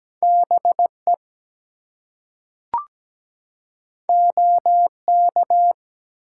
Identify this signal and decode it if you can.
Morse code, heard as a keyed tone, about 17 wpm, 700 Hz. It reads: BE OK
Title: Two Letter Words - 17wpm